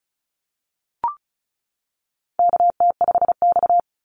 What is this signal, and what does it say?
Morse code, heard as a keyed tone, about 36 wpm, 700 Hz. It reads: KT5X
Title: NA SOTA Activator Callsigns 35wpm